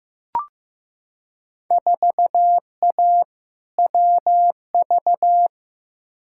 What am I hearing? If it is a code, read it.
Morse code, heard as a keyed tone, about 15 wpm, 700 Hz. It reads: 4A WV
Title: Field Day Exchanges Encore - 15wpm